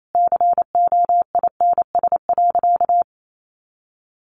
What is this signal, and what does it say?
Morse code, heard as a keyed tone, about 28 wpm, 700 Hz. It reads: COINS.
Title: A Sound Of Thunder 28wpm